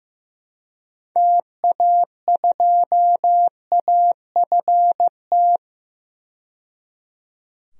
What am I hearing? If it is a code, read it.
Morse code, heard as a keyed tone, about 15 wpm, 700 Hz. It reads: TA2AFT
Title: DXCC-100 Call Signs - 6 Characters - 15wpm